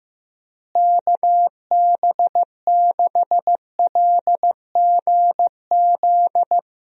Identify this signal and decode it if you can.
Morse code, heard as a keyed tone, about 15 wpm, 700 Hz. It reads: KB6LGZ
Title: USA Call Signs - 6 Characters - Repeated 2x - 15wpm